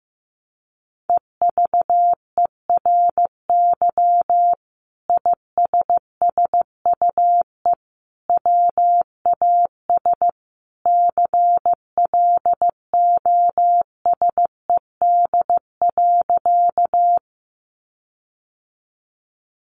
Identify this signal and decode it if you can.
Morse code, heard as a keyed tone, about 15 wpm, 700 Hz. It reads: EVERY ISSUE WAS CLOSED.
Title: Sentences from Top 1400 Words 15wpm